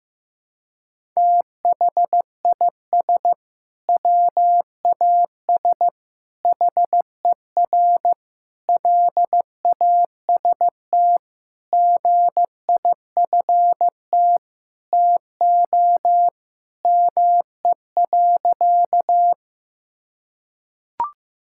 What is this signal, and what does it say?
Morse code, heard as a keyed tone, about 15 wpm, 700 Hz. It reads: THIS WAS HER LAST GIFT TO ME.
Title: Sentences from Top 1000 Words - Review Part 6 - 15wpm